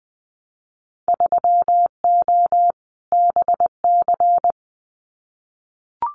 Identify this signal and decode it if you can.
Morse code, heard as a keyed tone, about 20 wpm, 700 Hz. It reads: 3O BC
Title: Winter Field Day Exchanges Encore 20wpm